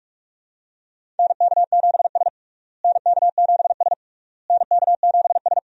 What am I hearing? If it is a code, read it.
Morse code, heard as a keyed tone, about 42 wpm, 700 Hz. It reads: NK7S NK7S NK7S
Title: USA Call Signs - 4 Characters - Repeated 3x - 45wpm